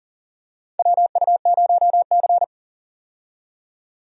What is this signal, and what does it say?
Morse code, heard as a keyed tone, about 37 wpm, 700 Hz. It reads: WU0C
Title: USA Call Signs - 4 Characters - 40wpm